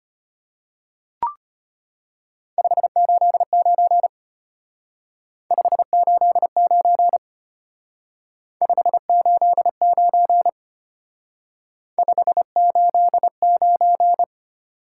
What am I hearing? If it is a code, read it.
Morse code, heard as a keyed tone, about 32 wpm, 700 Hz. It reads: 589 589 589 589